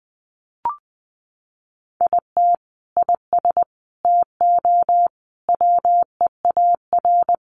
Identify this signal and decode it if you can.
Morse code, heard as a keyed tone, about 20 wpm, 700 Hz. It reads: IT IS TO WEAR
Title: Sentences from Top 1500 Words 20wpm